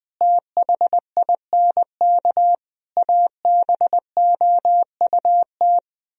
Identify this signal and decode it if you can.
Morse code, heard as a keyed tone, about 20 wpm, 700 Hz. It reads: THINK ABOUT
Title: Speed-Racing - Sets of 5 Words 17wpm